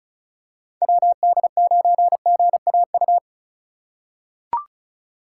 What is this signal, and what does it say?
Morse code, heard as a keyed tone, about 34 wpm, 700 Hz. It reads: WD9GAU
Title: USA Call Signs - 6 Characters - 3x Character Spacing - 35wpm